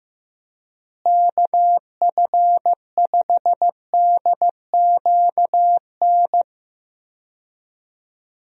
Morse code, heard as a keyed tone, about 15 wpm, 700 Hz. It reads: KF5DQN
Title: USA Call Signs - 6 Characters - Repeated 2x - 15wpm